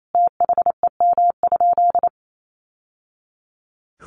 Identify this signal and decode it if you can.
Morse code, heard as a keyed tone, about 28 wpm, 700 Hz. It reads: THEM?
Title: Sentences from Top 800 Words 28wpm